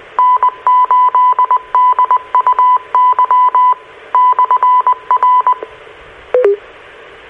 Morse code, heard as a keyed tone, about 20 wpm, 1000 Hz. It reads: N8DUY/R
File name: RR Skywarn / Washtenaw County SKYWARN N8DUY and W8SRC Repeaters